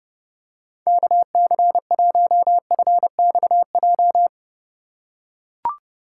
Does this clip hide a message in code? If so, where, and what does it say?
Yes: Morse code from 0.9 to 4.3 s, heard as a keyed tone, about 29 wpm, 700 Hz. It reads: KC1FXJ